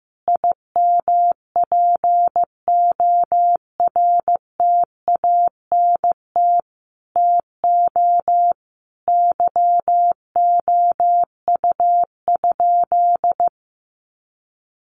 Morse code, heard as a keyed tone, about 15 wpm, 700 Hz. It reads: IMPORTANT TO YOU?
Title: Sentences from Top 1300 Words 15wpm